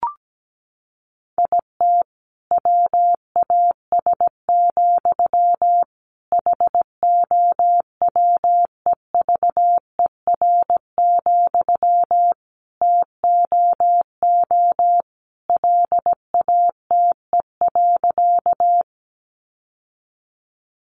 Morse code, heard as a keyed tone, about 17 wpm, 700 Hz. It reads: IT WAS, HOWEVER, TOO LATE.
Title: Sentences from Top 600 Words 17wpm